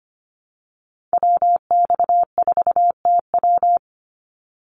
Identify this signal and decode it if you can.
Morse code, heard as a keyed tone, about 25 wpm, 700 Hz. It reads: WX4TW
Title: NA SOTA Activator Callsigns 25wpm